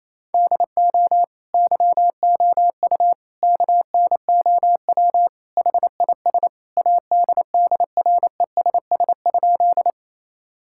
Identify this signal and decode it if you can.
Morse code, heard as a keyed tone, about 28 wpm, 700 Hz. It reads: DO YOU KNOW HIS ADDRESS?
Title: Sentences from Top 1000 Words - Review Part 6 - 28wpm